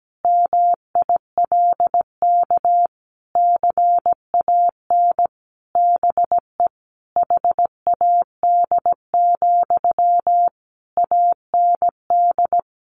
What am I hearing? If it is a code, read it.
Morse code, heard as a keyed tone, about 17 wpm, 700 Hz. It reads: MILK CAN BE HAD, AND
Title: Sentences from Top 1500 Words 17wpm